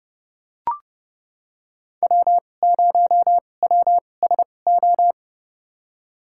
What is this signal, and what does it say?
Morse code, heard as a keyed tone, about 32 wpm, spaced as for 22 wpm, 700 Hz. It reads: W0WSO
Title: USA Call Signs - 5 Characters - 2x Character Spacing - 30wpm